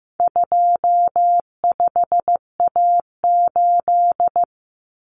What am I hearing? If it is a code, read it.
Morse code, heard as a keyed tone, about 15 wpm, 700 Hz. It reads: 25A8
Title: Common Cut Numbers - 4-Digit 15wpm